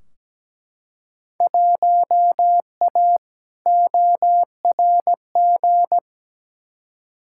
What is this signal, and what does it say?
Morse code, heard as a keyed tone, about 17 wpm, 700 Hz. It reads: 1A ORG
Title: Field Day Exchanges Encore - 17wpm